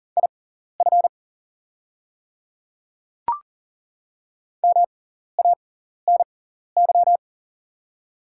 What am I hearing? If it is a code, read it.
Morse code, heard as a keyed tone, about 40 wpm, spaced as for 14 wpm, 690 Hz. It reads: IF MANY
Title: 6x Character Spacing - Top 200 Words - 40wpm